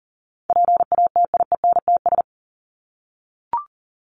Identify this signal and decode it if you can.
Morse code, heard as a keyed tone, about 40 wpm, 700 Hz. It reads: PATIENTS
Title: ES:S Suffix - Top 100 Words with Base Words 40wpm